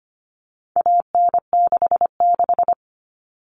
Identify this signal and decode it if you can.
Morse code, heard as a keyed tone, about 25 wpm, 700 Hz. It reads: AN66
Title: All Cut Numbers - 4-Digit 25wpm